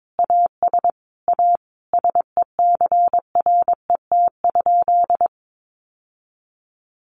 Morse code, heard as a keyed tone, about 22 wpm, 700 Hz. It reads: AS A SECRET?
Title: Sentences from Top 1200 Words 22wpm